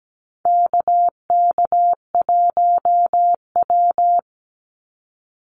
Morse code, heard as a keyed tone, about 17 wpm, 700 Hz. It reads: KK1W
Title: NA SOTA Activator Callsigns 17wpm